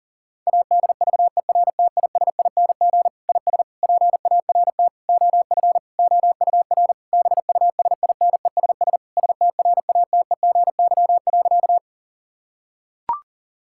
Morse code, heard as a keyed tone, about 40 wpm, 700 Hz. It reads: ADVERTISING IS PART OF OUR BUSINESS STRATEGY.
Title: Sentences from Top 1000 Words - Review Part 6 - 40wpm